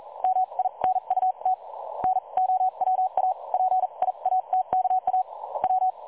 Morse code, heard as a keyed tone, about 39 wpm, spaced as for 33 wpm, 750 Hz. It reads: MINUT NOWA PIATKA W
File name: dlazaawansowanych / nagraniazpasma / websdr recording 3565.0kHz